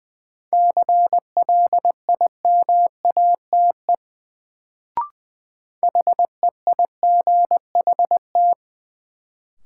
Morse code, heard as a keyed tone, about 20 wpm, 700 Hz. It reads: CLIMATE HEIGHT
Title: Sentences from Top 1600 Words - New Word Drill - 20wpm